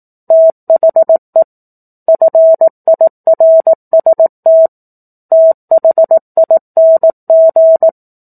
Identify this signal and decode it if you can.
Morse code, heard as a keyed tone, about 18 wpm, 650 Hz. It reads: THE FIRST THING
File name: 04-Chapter-04 18 WPM